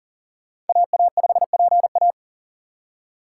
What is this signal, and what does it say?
Morse code, heard as a keyed tone, about 52 wpm, 700 Hz. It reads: AA5PA